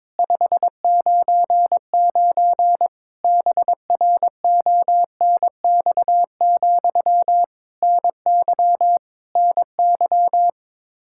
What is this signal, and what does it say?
Morse code, heard as a keyed tone, about 22 wpm, 700 Hz. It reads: 599 BRONX, NY NY
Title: Practice QSO - Exchange 1 - Beginning - 22wpm